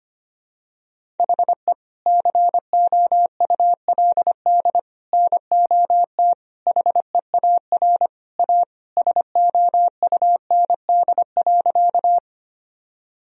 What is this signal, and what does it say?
Morse code, heard as a keyed tone, about 25 wpm, 700 Hz. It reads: HE COULD NOT HEAR A SOUND.
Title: Sentences from Top 1000 Words - Review Part 1 - 25wpm